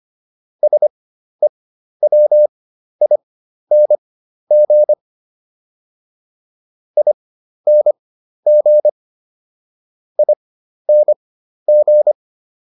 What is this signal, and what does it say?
Morse code, heard as a keyed tone, about 25 wpm, spaced as for 12 wpm, 600 Hz. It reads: SEWING ING ING